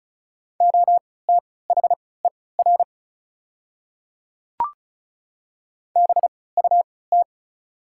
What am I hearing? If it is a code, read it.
Morse code, heard as a keyed tone, about 34 wpm, spaced as for 20 wpm, 700 Hz. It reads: OTHER BUT
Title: 3x Character Spacing - Top 100 Words - 35wpm